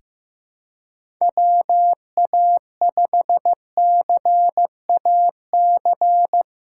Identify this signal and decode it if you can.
Morse code, heard as a keyed tone, about 15 wpm, 700 Hz. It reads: WA5CAC
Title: USA Callsign DE Callsign 15wpm